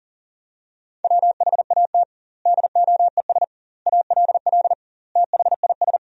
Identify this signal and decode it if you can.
Morse code, heard as a keyed tone, about 37 wpm, 700 Hz. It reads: WHAT DOES ALL THIS
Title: 2x Word-Spacing - Sentences from Top 200 Words - 40wpm